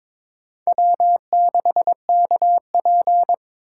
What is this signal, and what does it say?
Morse code, heard as a keyed tone, about 22 wpm, 700 Hz. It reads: W6KP
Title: USA Call Signs - 4 Characters - 4x Character Spacing - 22wpm